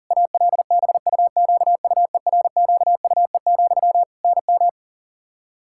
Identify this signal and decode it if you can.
Morse code, heard as a keyed tone, about 40 wpm, 700 Hz. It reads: ALBUQUERQUE, NM
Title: Practice QSO - Exchange 1 - Beginning - 40wpm